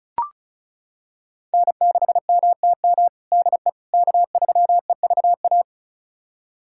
Morse code, heard as a keyed tone, about 34 wpm, 700 Hz. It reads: N6MTM DE K3EVA
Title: USA Callsign DE Callsign - 5 Characters - 35wpm